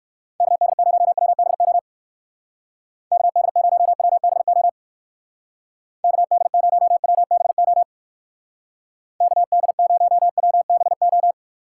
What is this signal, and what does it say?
Morse code, heard as a keyed tone, about 52 wpm, 700 Hz. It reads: KD0WBO KD0WBO KD0WBO KD0WBO
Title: Speed Racing - USA Call Signs 45wpm